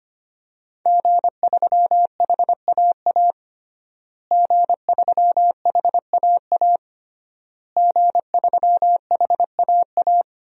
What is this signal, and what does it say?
Morse code, heard as a keyed tone, about 25 wpm, 700 Hz. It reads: G3HAA G3HAA G3HAA
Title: DXCC-100 Call Signs - 5 Characters - Repeated 3x - 25wpm